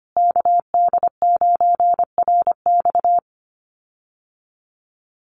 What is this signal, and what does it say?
Morse code, heard as a keyed tone, about 25 wpm, 700 Hz. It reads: KD9RX
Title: USA Call Signs Repeated 3x 25wpm